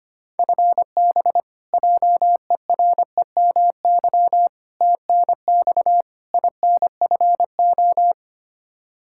Morse code, heard as a keyed tone, about 25 wpm, 700 Hz. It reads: FB JEREMY TNX INFO
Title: Practice QSO - Exchange 3 - More Info - 25wpm